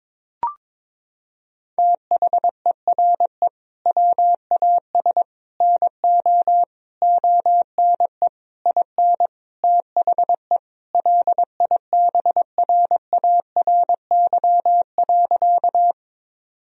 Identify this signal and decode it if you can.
Morse code, heard as a keyed tone, about 22 wpm, 700 Hz. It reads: THERE WAS NO ONE IN THE LIBRARY.
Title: Sentences from Top 1000 Words - Review Part 4 - 22wpm